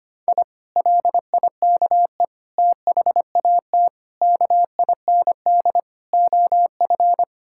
Morse code, heard as a keyed tone, about 25 wpm, 700 Hz. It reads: I LIKE THAT KIND OF
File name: practice / 25wpm / Sentences with 3-Word Phrases - 25wpm